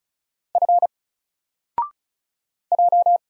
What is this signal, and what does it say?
Morse code, heard as a keyed tone, about 36 wpm, 700 Hz. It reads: F J
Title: Single Letter-Number Alt 35wpm